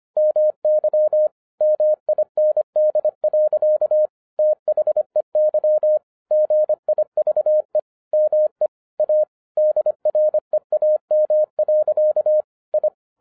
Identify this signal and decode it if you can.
Morse code, heard as a keyed tone, about 25 wpm, 610 Hz. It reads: MY MIND. THEY GIVE ME A DREAM. I
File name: 2011-07-mp3 / n24 QRP OTAKU 25wpm